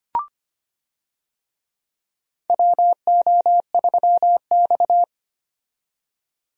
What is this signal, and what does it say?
Morse code, heard as a keyed tone, about 25 wpm, 700 Hz. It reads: WO3X
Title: USA Call Signs - 4 Characters - 25wpm